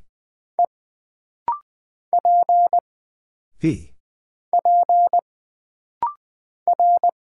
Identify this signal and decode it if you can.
Morse code, heard as a keyed tone, about 20 wpm, 700 Hz. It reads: E P P R
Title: Single Character (Letter:Number:Punctuation:Prosign) - Rapid-Fire - 20wpm